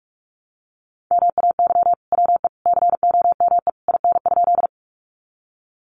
Morse code, heard as a keyed tone, about 44 wpm, 700 Hz. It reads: MAY WE COME IN?
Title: Sentences from Top 100 Words 45wpm